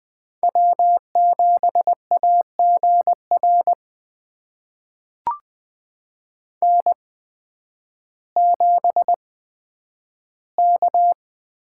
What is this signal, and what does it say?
Morse code, heard as a keyed tone, about 20 wpm, spaced as for 11 wpm, 700 Hz. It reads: W7AGR N7K
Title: USA Call Signs - 5 Characters - 8x Character Spacing - 20wpm